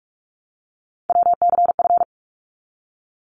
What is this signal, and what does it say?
Morse code, heard as a keyed tone, about 44 wpm, 700 Hz. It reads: WCF
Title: ARRL Sections 45wpm